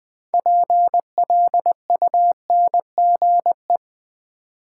Morse code, heard as a keyed tone, about 20 wpm, 700 Hz. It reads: PLUNGE
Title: Build A Word - 3 Letters At A Time - 6-Letter Words - 20wpm